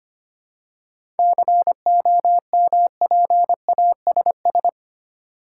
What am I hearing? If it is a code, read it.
Morse code, heard as a keyed tone, about 29 wpm, 700 Hz. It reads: COMPASS
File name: practice / 25wpm / Difficult Words - Speed-Racing with Hints - 25wpm